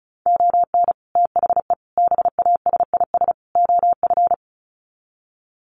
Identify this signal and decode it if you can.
Morse code, heard as a keyed tone, about 34 wpm, 700 Hz. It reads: ON THE BASIS OF
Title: Sets of 4 Words 35wpm